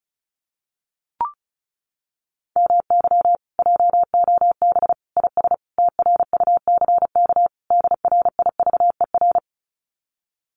Morse code, heard as a keyed tone, about 34 wpm, 700 Hz. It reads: MY JOB IS TRUCK DRIVER
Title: Common QSO Phrases Encore 35wpm